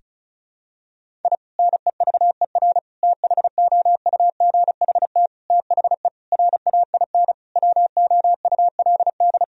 Morse code, heard as a keyed tone, about 36 wpm, 700 Hz. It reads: I NEVER THOUGHT THE RAIN WOULD
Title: Sentences from Top 1000 Words - Review Part 1 - 35wpm